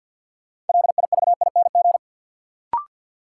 Morse code, heard as a keyed tone, about 52 wpm, 700 Hz. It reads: LIVING
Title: Sets of 4 Words - New Word Drill - 50wpm